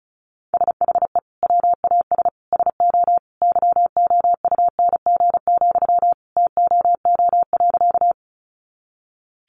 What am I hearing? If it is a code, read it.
Morse code, heard as a keyed tone, about 34 wpm, 700 Hz. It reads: SHE WAS SO YOUNG, TOO.
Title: Sentences from Top 400 Words 35wpm